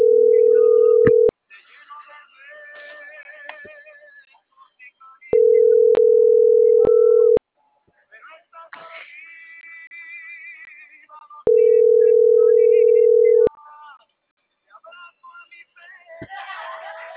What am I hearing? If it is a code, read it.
Ringback tone (North America), 440 + 480 Hz.